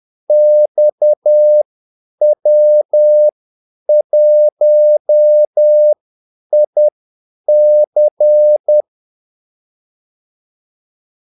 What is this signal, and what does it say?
Morse code, heard as a keyed tone, about 10 wpm, spaced as for 8 wpm, 600 Hz. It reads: XW1IC
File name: Callsigns-10-8wpm